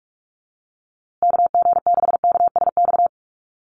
Morse code, heard as a keyed tone, about 44 wpm, 700 Hz. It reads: KG6KSX
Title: USA Call Signs Repeated 3x 45wpm